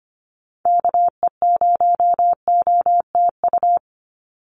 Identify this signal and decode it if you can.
Morse code, heard as a keyed tone, about 25 wpm, 700 Hz. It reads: KE0OTU
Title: USA Call Signs 25wpm